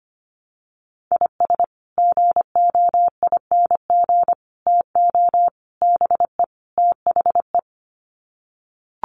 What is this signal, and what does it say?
Morse code, heard as a keyed tone, about 25 wpm, 700 Hz. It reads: IS GOING TO BE THE
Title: Sets of 5 Words 25wpm